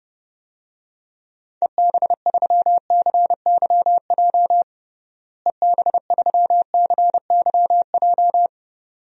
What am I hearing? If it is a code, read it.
Morse code, heard as a keyed tone, about 30 wpm, 700 Hz. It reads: EB3CYJ EB3CYJ
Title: DXCC-100 Call Signs - 6 Characters - Repeated 2x - 30wpm